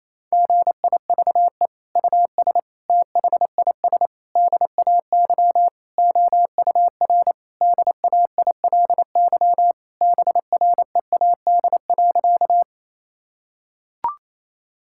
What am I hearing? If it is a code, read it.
Morse code, heard as a keyed tone, about 28 wpm, 700 Hz. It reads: GIVE US THIS DAY OUR DAILY BREAD.
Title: Sentences from Top 1600 Words 28wpm